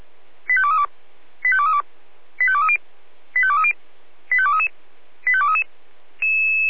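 Five-tone selective call (ZVEI), heard as quick runs of five tones, tones of about 70 ms. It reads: 85232 85232 85230 85230 85233 85233
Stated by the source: Received by SDR, ZVEI five-tone selective call.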